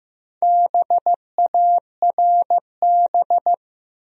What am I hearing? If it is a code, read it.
Morse code, heard as a keyed tone, about 15 wpm, 700 Hz. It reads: BARB
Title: USA Common Names - Max 4 Letters - 15wpm